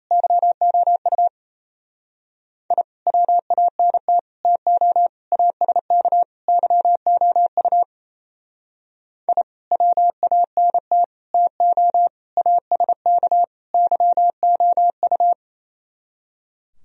Morse code, heard as a keyed tone, about 35 wpm, 700 Hz. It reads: YOU I WANT TO ASK YOU I WANT TO ASK YOU